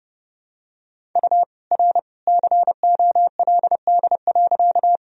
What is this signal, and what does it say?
Morse code, heard as a keyed tone, about 30 wpm, 700 Hz. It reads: U R COLD.
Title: Sentences from Top 400 Words and Abbreviations 30wpm